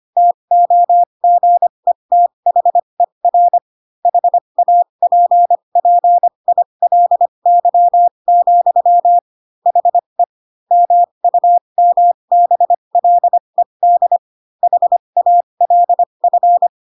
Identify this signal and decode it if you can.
Morse code, heard as a keyed tone, about 25 wpm, spaced as for 22 wpm, 700 Hz. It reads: TOGETHER HAPPILY, HE MUMBLED HALF